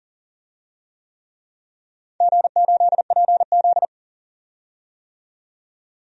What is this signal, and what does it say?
Morse code, heard as a keyed tone, about 40 wpm, 700 Hz. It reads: G8PZ
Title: DXCC-100 Call Signs - 4 Characters - 40wpm